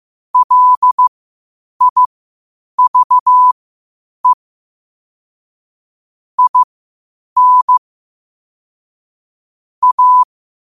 Morse code, heard as a keyed tone, about 15 wpm, spaced as for 8 wpm, 1000 Hz. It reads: LIVE IN A
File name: qotdfiles / 8wpm